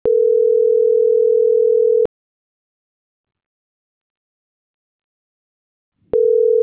Ringback tone (North America), 440 + 480 Hz.
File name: CENTRAL MAERSK / out nogueronmsk